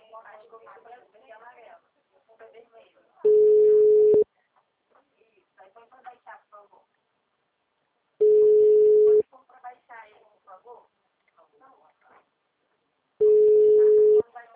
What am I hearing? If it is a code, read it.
Ringback tone (Europe, 425 Hz), with speech.
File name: out- -2004- - - .14105